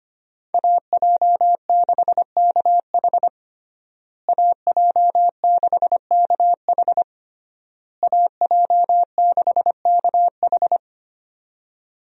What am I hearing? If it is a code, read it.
Morse code, heard as a keyed tone, about 25 wpm, 700 Hz. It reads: AJ6KH AJ6KH AJ6KH
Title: USA Call Signs - 5 Characters - Repeated 3x - 25wpm